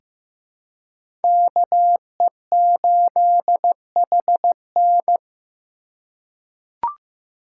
Morse code, heard as a keyed tone, about 15 wpm, 700 Hz. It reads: KE8HN
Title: USA Call Signs - 4x Character Spacing - 15wpm